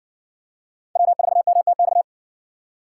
Morse code, heard as a keyed tone, about 57 wpm, 700 Hz. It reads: W4MTX